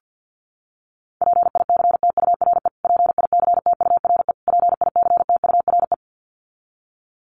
Morse code, heard as a keyed tone, about 50 wpm, 700 Hz. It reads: PICTURE PICTURE PICTURE
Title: Top 300 Words Repeated 50wpm